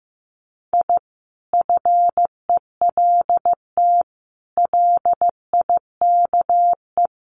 Morse code, heard as a keyed tone, about 15 wpm, 700 Hz. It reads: I FELT LIKE
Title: Sets of 5 Words Encore 15wpm